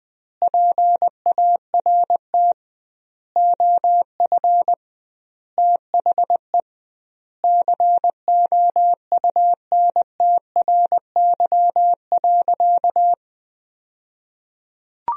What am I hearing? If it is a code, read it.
Morse code, heard as a keyed tone, about 21 wpm, 700 Hz. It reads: PART OF THE COUNTRY.
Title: 2x Word-Spacing - Sentences from Top 300 Words - 20wpm